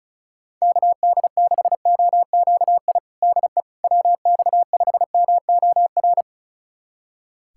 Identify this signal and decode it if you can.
Morse code, heard as a keyed tone, about 34 wpm, 700 Hz. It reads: KD6OQI DE WX5MOR
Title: USA Callsign DE Callsign - 6 Characters - 35wpm